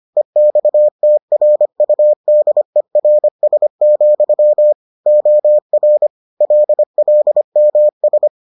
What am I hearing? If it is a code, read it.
Morse code, heard as a keyed tone, about 25 wpm, 600 Hz. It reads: EXTRUDERS, OR LLMS